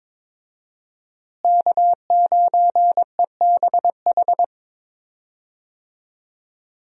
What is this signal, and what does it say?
Morse code, heard as a keyed tone, about 22 wpm, 700 Hz. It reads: K9EBH
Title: USA Call Signs - 5 Characters - Repeated 2x - 22wpm